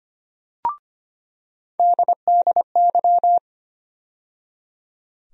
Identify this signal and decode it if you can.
Morse code, heard as a keyed tone, about 25 wpm, 700 Hz. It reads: DDY
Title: USA Call Sign Postfixes - 3 Letters 25wpm